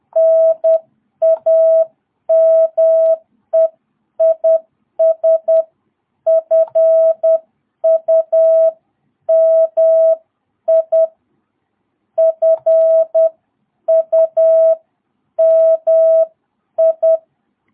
Morse code, heard as a keyed tone, about 10 wpm, spaced as for 9 wpm, 660 Hz. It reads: NAME IS FUMI FUMI
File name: jh8cbh / 2022cw